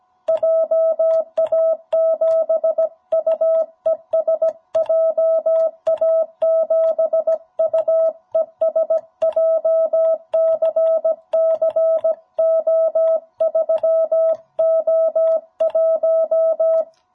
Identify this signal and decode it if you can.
Morse code, heard as a keyed tone, about 17 wpm, 650 Hz. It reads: JA7UES JA7UES JCC O3O1
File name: jh8cbh / 2021cw